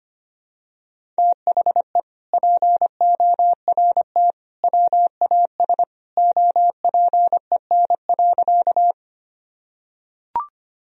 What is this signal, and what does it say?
Morse code, heard as a keyed tone, about 25 wpm, 700 Hz. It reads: THE PORT WAS OPEN.
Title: Sentences from Top 1000 Words - Review Part 1 - 25wpm